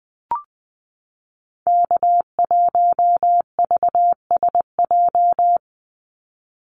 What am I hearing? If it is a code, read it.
Morse code, heard as a keyed tone, about 19 wpm, 700 Hz. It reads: K1VSJ
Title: Call Signs - 20wpm